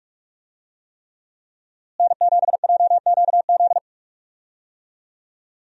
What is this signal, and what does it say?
Morse code, heard as a keyed tone, about 42 wpm, 700 Hz. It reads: N7JQZ
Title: USA Call Signs - 5 Characters - Repeated 3x - 45wpm